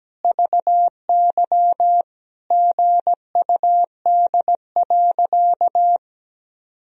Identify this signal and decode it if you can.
Morse code, heard as a keyed tone, about 17 wpm, 700 Hz. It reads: VY GUD.
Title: Sentences from Top 200 Words and Abbreviations 17wpm